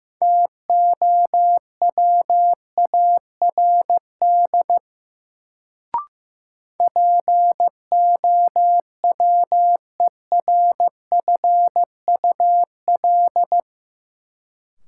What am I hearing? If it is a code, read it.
Morse code, heard as a keyed tone, about 16 wpm, 700 Hz. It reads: TOWARD POWERFUL
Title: Sentences from Top 1300 Words - New Word Drill Alt - 15wpm